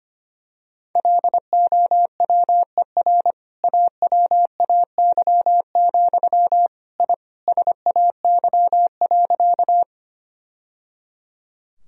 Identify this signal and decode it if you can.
Morse code, heard as a keyed tone, about 25 wpm, 700 Hz. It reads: LOWER AWAY, I SAY.
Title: Sentences from Top 1000 Words - Review Part 4 - 25wpm